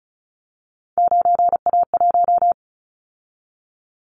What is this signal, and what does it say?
Morse code, heard as a keyed tone, about 36 wpm, 700 Hz. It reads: 9A1
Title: Callsign Prefixes for Top 25 DXCC - 35wpm